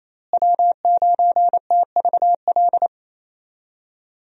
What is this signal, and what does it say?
Morse code, heard as a keyed tone, about 28 wpm, 700 Hz. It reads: W9TVL
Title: USA Call Signs - 5 Characters - Repeated 2x - 28wpm